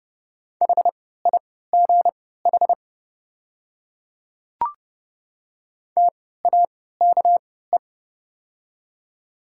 Morse code, heard as a keyed tone, about 31 wpm, spaced as for 17 wpm, 690 Hz. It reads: HIGH TAKE
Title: 3x Character Spacing - Top 200 Words - 30wpm